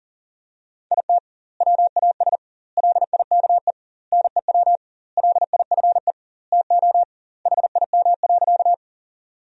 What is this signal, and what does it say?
Morse code, heard as a keyed tone, about 40 wpm, 700 Hz. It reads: IT WAS LIKE NEW LIFE TO HIM.
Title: 2x Word-Spacing - Sentences from Top 300 Words - 40wpm